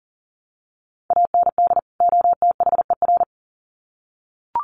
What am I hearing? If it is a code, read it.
Morse code, heard as a keyed tone, about 40 wpm, 700 Hz. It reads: AND OTHER
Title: Sets of 2 Words 40wpm